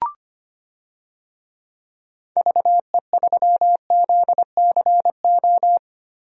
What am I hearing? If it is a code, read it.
Morse code, heard as a keyed tone, about 25 wpm, 700 Hz. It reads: VE3ZCO
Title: DXCC-100 Call Signs - 6 Characters - 25wpm